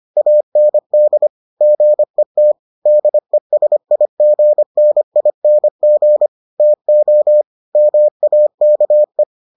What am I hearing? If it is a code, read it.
Morse code, heard as a keyed tone, about 25 wpm, 600 Hz. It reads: AND GET DESIGNING TO MAKE